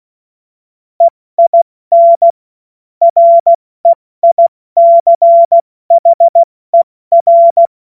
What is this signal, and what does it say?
Morse code, heard as a keyed tone, about 16 wpm, 680 Hz. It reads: EIN REICHER
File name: Morsekurs / Witze / Witz 07. Februar 2023 70 F80